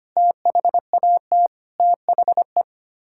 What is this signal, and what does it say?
Morse code, heard as a keyed tone, about 25 wpm, 700 Hz. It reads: THAT THE
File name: Sets of 9 Words - 25wpm